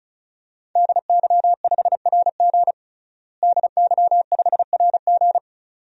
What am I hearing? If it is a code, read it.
Morse code, heard as a keyed tone, about 36 wpm, 700 Hz. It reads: DY5RG DY5RG
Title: DXCC-100 Call Signs - 5 Characters - Repeated 2x - 35wpm